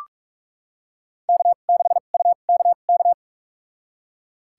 Morse code, heard as a keyed tone, about 46 wpm, spaced as for 33 wpm, 700 Hz. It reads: K6UKK
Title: USA Call Signs - 2x Character Spacing - 45wpm